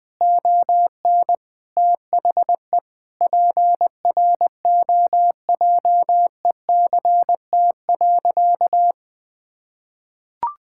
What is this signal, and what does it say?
Morse code, heard as a keyed tone, about 21 wpm, 700 Hz. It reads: ON THE PROJECT.
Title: Sentences with 9-Word Phrases - 20wpm